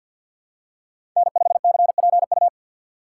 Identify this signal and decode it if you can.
Morse code, heard as a keyed tone, about 53 wpm, 700 Hz. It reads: N5CPU
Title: USA Call Signs - 5 Characters - 6x Character Spacing - 50wpm